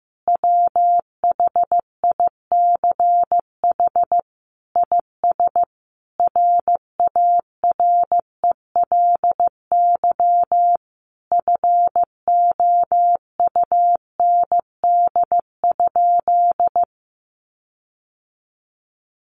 Morse code, heard as a keyed tone, about 15 wpm, 700 Hz. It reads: WHICH IS RARELY FOUND?
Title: Sentences from Top 1400 Words 15wpm